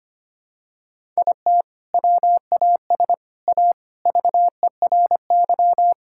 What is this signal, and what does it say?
Morse code, heard as a keyed tone, about 25 wpm, 700 Hz. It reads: IT WAS A VERY
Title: Sentences from Top 1000 Words - Review Part 5 - 25wpm